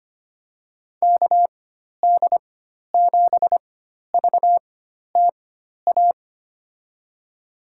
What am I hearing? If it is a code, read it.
Morse code, heard as a keyed tone, about 25 wpm, spaced as for 12 wpm, 700 Hz. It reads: KD7VTA
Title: USA Call Signs - 4x Character Spacing - 25wpm